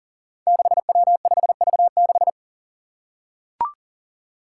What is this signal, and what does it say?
Morse code, heard as a keyed tone, about 40 wpm, 700 Hz. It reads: 6W5V6
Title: Five Challenging Letter-Number Combinations 40wpm